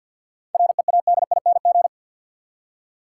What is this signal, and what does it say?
Morse code, heard as a keyed tone, about 46 wpm, 700 Hz. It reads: READING
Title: Seven Letter Words - 50wpm